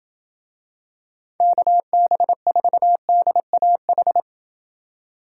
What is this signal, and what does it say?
Morse code, heard as a keyed tone, about 28 wpm, 700 Hz. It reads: KB4DAH